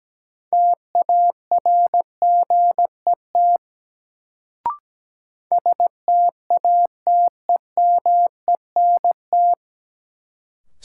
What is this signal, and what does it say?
Morse code, heard as a keyed tone, about 17 wpm, 700 Hz. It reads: TARGET STATEMENT
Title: Sentences from Top 1200 Words - New Word Drill - 17wpm